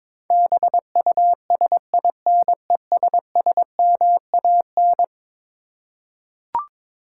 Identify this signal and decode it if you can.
Morse code, heard as a keyed tone, about 22 wpm, 700 Hz. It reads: BUSINESSMAN
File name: practice / 22wpm / Eleven Letter Words - 22wpm